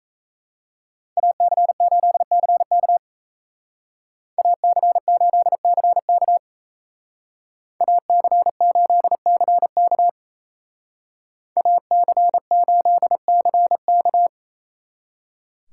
Morse code, heard as a keyed tone, about 36 wpm, 700 Hz. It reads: AC8CK AC8CK AC8CK AC8CK